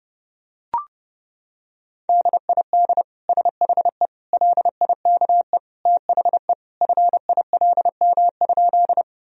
Morse code, heard as a keyed tone, about 32 wpm, 700 Hz. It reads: DID SHE LIKE THE FILM?
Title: Sentences from Top 1000 Words - Review Part 3 - 30wpm